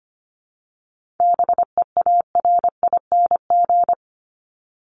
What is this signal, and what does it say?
Morse code, heard as a keyed tone, about 25 wpm, 700 Hz. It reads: BEARING
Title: ING Suffix - Top 100 Words with Base Words 25wpm